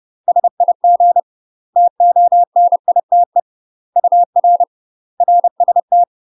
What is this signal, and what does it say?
Morse code, heard as a keyed tone, about 30 wpm, 700 Hz. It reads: SIG TONITE UR RST
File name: UK MASTER QSOs TRAINING / 30WPM / UK MASTER QSOs 0018